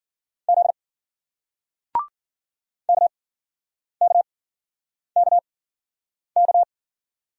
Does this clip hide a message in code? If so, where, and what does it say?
Yes: Morse code from 0.5 to 6.6 s, heard as a keyed tone, about 51 wpm, spaced as for 9 wpm, 700 Hz. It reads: Z KKKK